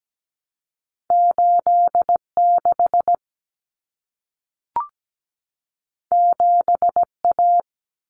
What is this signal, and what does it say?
Morse code, heard as a keyed tone, about 17 wpm, 700 Hz. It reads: 86 7A
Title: Common Cut Numbers - 2-Digit 17wpm